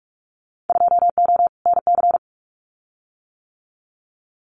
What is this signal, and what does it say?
Morse code, heard as a keyed tone, about 44 wpm, 700 Hz. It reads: 2O NC
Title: Winter Field Day Exchanges 45wpm